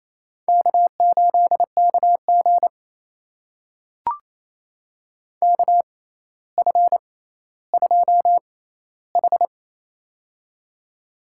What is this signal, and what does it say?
Morse code, heard as a keyed tone, about 27 wpm, spaced as for 14 wpm, 700 Hz. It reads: K8KG KF2H
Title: USA Call Signs - 4 Characters - 6x Character Spacing - 28wpm